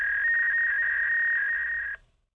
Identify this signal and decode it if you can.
Fax machine handshake.
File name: machine sound effects / fax call